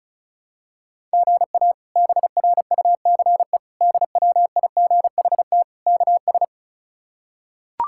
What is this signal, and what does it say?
Morse code, heard as a keyed tone, about 34 wpm, 700 Hz. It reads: GA BRUCE DWIGHT KS
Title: K1USN SST QSO Practice - 35wpm